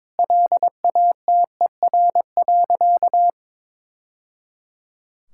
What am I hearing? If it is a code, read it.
Morse code, heard as a keyed tone, about 22 wpm, 700 Hz. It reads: LATER.
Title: Sentences from Top 1000 Words - Review Part 5 - 22wpm